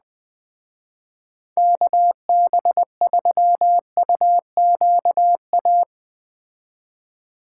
Morse code, heard as a keyed tone, about 20 wpm, 700 Hz. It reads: KB3UQA
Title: Speed Racing - USA 2x3 Call Signs 17wpm